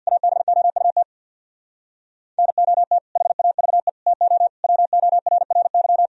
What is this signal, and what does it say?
Morse code, heard as a keyed tone, about 64 wpm, 700 Hz. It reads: ABOUT NOT HAVE TO WORRY